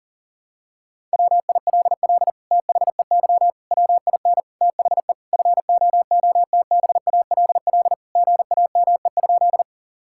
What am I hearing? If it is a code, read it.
Morse code, heard as a keyed tone, about 40 wpm, 700 Hz. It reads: WILL THEY WIN THE FOOTBALL GAME?
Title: Sentences from Top 1600 Words 40wpm